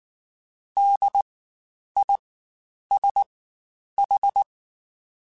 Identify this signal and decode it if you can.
Morse code, heard as a keyed tone, about 19 wpm, spaced as for 9 wpm, 800 Hz. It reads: DISH
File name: Spaced Alphas / Spaced Alphas mp3s / Ds sp @18